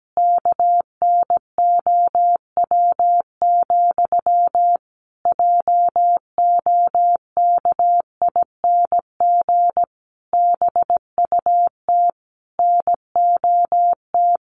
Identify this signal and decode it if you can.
Morse code, heard as a keyed tone, about 17 wpm, 700 Hz. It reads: KNOW, JOKING BUT NOT
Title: A Sound Of Thunder 17wpm